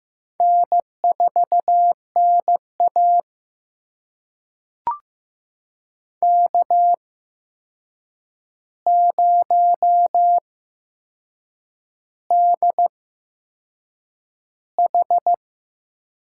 Morse code, heard as a keyed tone, about 15 wpm, spaced as for 6 wpm, 700 Hz. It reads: N4NA K0DH
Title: USA Call Signs - 4 Characters - 8x Character Spacing - 15wpm